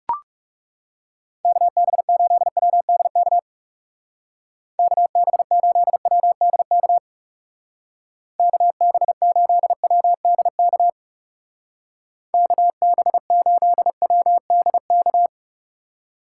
Morse code, heard as a keyed tone, about 37 wpm, 700 Hz. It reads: KB8WDK KB8WDK KB8WDK KB8WDK
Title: Speed Racing - USA 2x3 Call Signs 30wpm